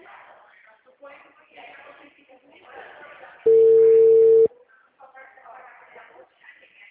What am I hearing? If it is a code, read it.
Ringback tone (China, 450 Hz), with speech.
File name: out- -2045- - - .18293